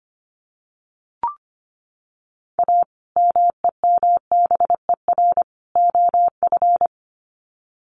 Morse code, heard as a keyed tone, about 25 wpm, 700 Hz. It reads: A MEMBER OF
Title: Sets of 3 Words 25wpm